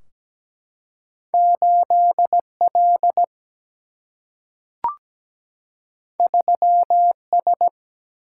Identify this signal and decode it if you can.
Morse code, heard as a keyed tone, about 17 wpm, 700 Hz. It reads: 8L 3S
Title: Single-Letter Single-Number 17wpm